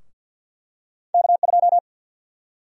Morse code, heard as a keyed tone, about 46 wpm, 700 Hz. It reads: K2
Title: USA Call Sign Prefixes - Repeated 2x - 50wpm